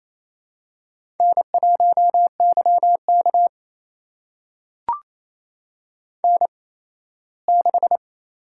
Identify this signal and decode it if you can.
Morse code, heard as a keyed tone, about 28 wpm, spaced as for 17 wpm, 700 Hz. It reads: N1YK N6
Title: USA Call Signs - 4 Characters - 8x Character Spacing - 28wpm